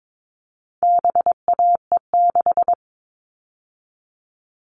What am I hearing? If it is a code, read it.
Morse code, heard as a keyed tone, about 22 wpm, 700 Hz. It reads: BAE6
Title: All Cut Numbers - 4-Digit 22wpm